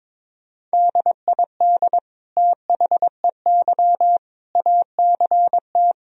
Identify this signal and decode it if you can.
Morse code, heard as a keyed tone, about 22 wpm, 700 Hz. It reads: DID THEY ACT
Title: Sentences from Top 1000 Words - Review Part 4 - 22wpm